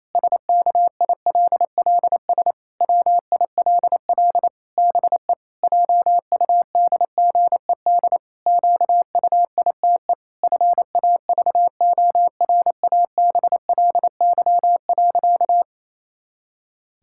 Morse code, heard as a keyed tone, about 28 wpm, 700 Hz. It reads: SKILLS WILL BE JUDGED QUITE FAVORABLY.
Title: Pangram Challenge v2 - 28wpm